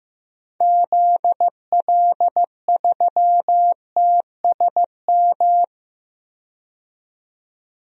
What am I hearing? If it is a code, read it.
Morse code, heard as a keyed tone, about 15 wpm, 700 Hz. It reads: ZL3TSM
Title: DXCC-100 Call Signs - 6 Characters - 15wpm